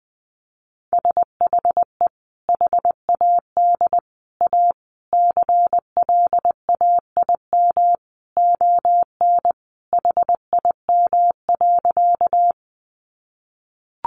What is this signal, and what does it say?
Morse code, heard as a keyed tone, about 20 wpm, 700 Hz. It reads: SHE HAD A CLAIM ON HIM.
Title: Sentences from Top 1500 Words 20wpm